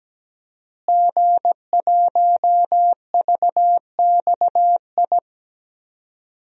Morse code, heard as a keyed tone, about 17 wpm, 700 Hz. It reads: G1VXI
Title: DXCC-100 Call Signs - 5 Characters - Repeated 3x - 17wpm